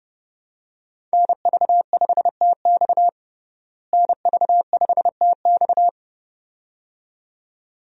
Morse code, heard as a keyed tone, about 30 wpm, 700 Hz. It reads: NV5TX NV5TX
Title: USA Call Signs - 5 Characters - Repeated 2x - 30wpm